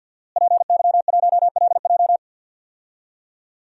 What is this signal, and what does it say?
Morse code, heard as a keyed tone, about 51 wpm, 700 Hz. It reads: PY1LJ
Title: DXCC-100 Call Signs - 5 Characters - Repeated 2x - 50wpm